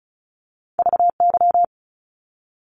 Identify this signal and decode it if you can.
Morse code, heard as a keyed tone, about 34 wpm, 700 Hz. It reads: VY
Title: Abbreviations 35wpm